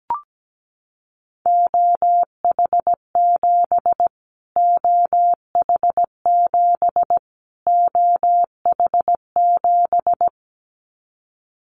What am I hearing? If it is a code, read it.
Morse code, heard as a keyed tone, about 17 wpm, 700 Hz. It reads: OH7 OH7 OH7
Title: Callsign Prefixes for Top 25 DXCC - Repeated 3x - 17wpm